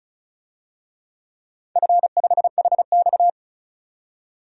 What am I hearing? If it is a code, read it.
Morse code, heard as a keyed tone, about 36 wpm, 700 Hz. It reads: F5HX
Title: DXCC-100 Call Signs - 4 Characters - 35wpm